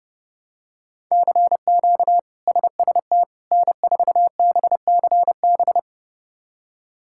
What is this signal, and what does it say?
Morse code, heard as a keyed tone, about 30 wpm, 700 Hz. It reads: CQ SST N4BCB
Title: K1USN SST QSO Practice - 30wpm